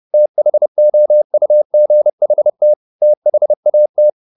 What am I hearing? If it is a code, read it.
Morse code, heard as a keyed tone, about 30 wpm, 600 Hz. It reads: THOUGHT THAT